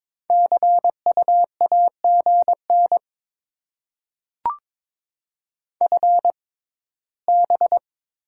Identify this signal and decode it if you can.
Morse code, heard as a keyed tone, about 22 wpm, spaced as for 15 wpm, 690 Hz. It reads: CU AGN FB
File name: Top 20 QSO Elements - 6x Character Spacing - 22wpm